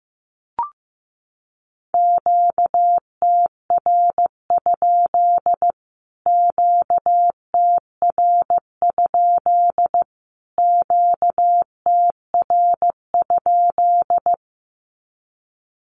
Morse code, heard as a keyed tone, about 15 wpm, 700 Hz. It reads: QTR? QTR? QTR?
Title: Q-Codes Repeated 3x 15wpm